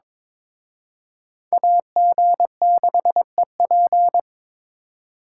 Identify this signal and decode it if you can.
Morse code, heard as a keyed tone, about 26 wpm, 700 Hz. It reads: AG6EP
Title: Speed Racing - USA 2x2 Call Signs 22wpm